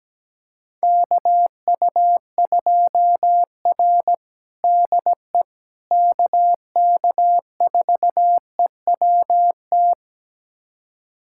Morse code, heard as a keyed tone, about 17 wpm, 700 Hz. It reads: KU2R DE KK4EWT
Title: USA Callsign DE Callsign 17wpm